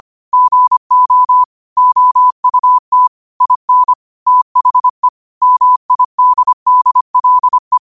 Morse code, heard as a keyed tone, about 25 wpm, 1000 Hz. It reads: GO OUT IN THE MIDDLE
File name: qotdfiles / 25wpm